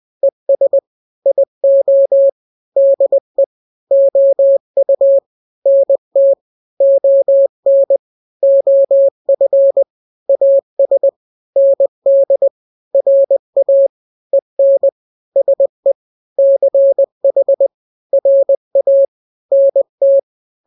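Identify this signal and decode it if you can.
Morse code, heard as a keyed tone, about 20 wpm, 550 Hz. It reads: ES IO DE OU NT ON OF AS ND RA EN SE CH RA NT
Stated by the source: Morse drill of two-letter n-grams at 20/18 WPM (Farnsworth), from KK7RR.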